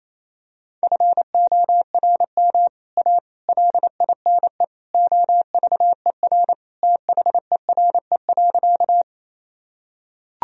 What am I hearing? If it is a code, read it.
Morse code, heard as a keyed tone, about 28 wpm, 700 Hz. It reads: FORM A LINE OVER THERE.
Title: Sentences from Top 1000 Words - Review Part 1 - 28wpm